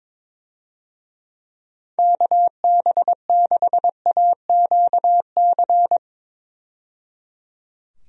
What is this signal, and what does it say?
Morse code, heard as a keyed tone, about 22 wpm, 700 Hz. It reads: KB6AQC
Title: USA Call Signs - 6 Characters - 22wpm